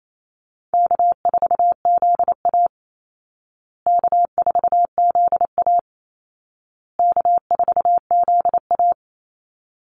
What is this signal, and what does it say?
Morse code, heard as a keyed tone, about 28 wpm, 700 Hz. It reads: K4ZA K4ZA K4ZA
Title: Easy Call Signs Repeated 3x - 28wpm